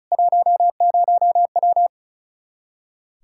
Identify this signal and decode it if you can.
Morse code, heard as a keyed tone, about 36 wpm, 700 Hz. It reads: 10W
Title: Practice QSO - Exchange 2 - Info - 35wpm